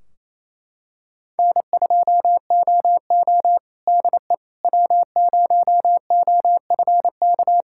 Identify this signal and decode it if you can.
Morse code, heard as a keyed tone, about 28 wpm, 700 Hz. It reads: N2OO DE W0OFK
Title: K1USN SST QSO Practice 28wpm